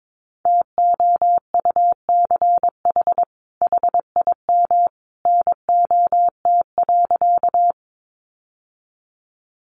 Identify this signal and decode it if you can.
Morse code, heard as a keyed tone, about 22 wpm, 700 Hz. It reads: TOUCH HIM NOT.
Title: Sentences from Top 1200 Words 22wpm